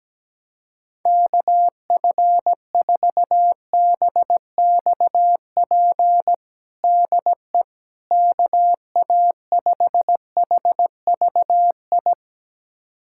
Morse code, heard as a keyed tone, about 17 wpm, 700 Hz. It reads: KF4BXP DE KA5HVI
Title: USA Callsign DE Callsign - 6 Characters - 17wpm